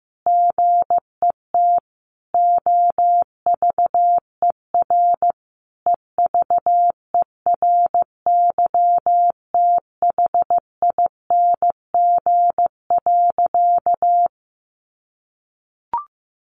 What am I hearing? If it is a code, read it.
Morse code, heard as a keyed tone, about 15 wpm, 700 Hz. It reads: GET OVER EVERYTHING.
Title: Sentences from Top 900 Words 15wpm